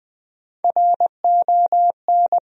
Morse code, heard as a keyed tone, about 20 wpm, 700 Hz. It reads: RON
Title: USA Common Names - Max 4 Letters - 20wpm